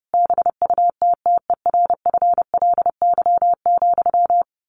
Morse code, heard as a keyed tone, about 28 wpm, 700 Hz. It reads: BUTTERFLY,
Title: A Sound Of Thunder 30wpm